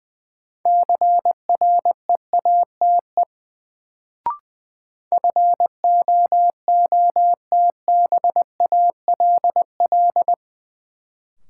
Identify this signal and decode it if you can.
Morse code, heard as a keyed tone, about 20 wpm, 700 Hz. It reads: CREATE FOOTBALL
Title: Sentences from Top 1600 Words - New Word Drill Alt - 20wpm